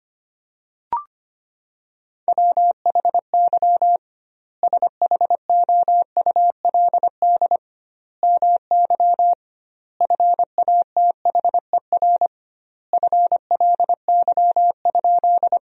Morse code, heard as a keyed tone, about 25 wpm, 700 Hz. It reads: WHY SHOULD MY FATHER FLY?
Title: 2x Word-Spacing - Sentences from Top 500 Words - 25wpm